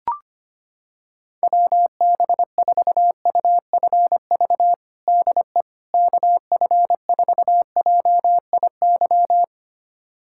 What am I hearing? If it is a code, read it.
Morse code, heard as a keyed tone, about 25 wpm, 700 Hz. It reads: WB4UFV DE KF4JIY
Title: USA Callsign DE Callsign - 6 Characters - 25wpm